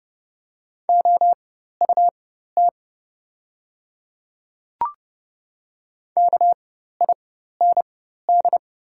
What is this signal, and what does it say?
Morse code, heard as a keyed tone, about 30 wpm, spaced as for 14 wpm, 700 Hz. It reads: OUT KIND
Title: 4x Character Spacing - Top 300 Words - 30wpm